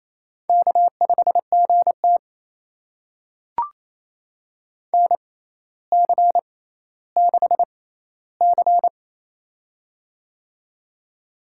Morse code, heard as a keyed tone, about 28 wpm, spaced as for 14 wpm, 700 Hz. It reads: K5GT NC6C
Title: USA Call Signs - 6x Character Spacing - 28wpm